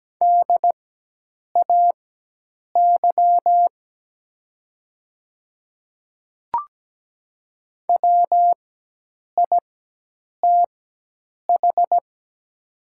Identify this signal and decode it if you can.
Morse code, heard as a keyed tone, about 17 wpm, spaced as for 8 wpm, 700 Hz. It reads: DAY WITH
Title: 4x Character Spacing - Top 100 Words - 17wpm